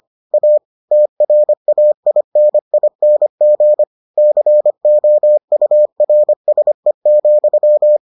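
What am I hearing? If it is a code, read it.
Morse code, heard as a keyed tone, about 25 wpm, 600 Hz. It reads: A TRAINING COURSE,
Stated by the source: Morse edition of the Foundations of Amateur Radio episodes.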